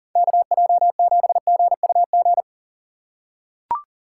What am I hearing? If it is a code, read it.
Morse code, heard as a keyed tone, about 40 wpm, 700 Hz. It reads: KJ7GUG
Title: USA Call Signs - 6 Characters - 4x Character Spacing - 40wpm